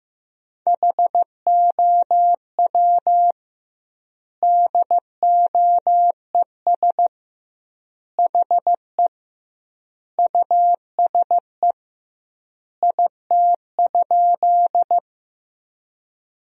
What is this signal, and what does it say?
Morse code, heard as a keyed tone, about 15 wpm, 700 Hz. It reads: HOW DOES HE USE IT?
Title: 2x Word-Spacing - Sentences from Top 200 Words - 15wpm